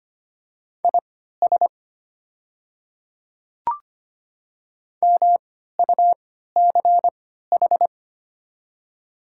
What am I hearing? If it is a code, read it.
Morse code, heard as a keyed tone, about 25 wpm, spaced as for 14 wpm, 700 Hz. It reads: IS MUCH
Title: 3x Character Spacing - Top 300 Words - 25wpm